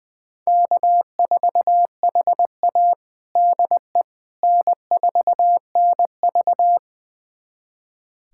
Morse code, heard as a keyed tone, about 20 wpm, 700 Hz. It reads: K4HA DE N4NV
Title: USA Callsign DE Callsign - 4 Characters - 20wpm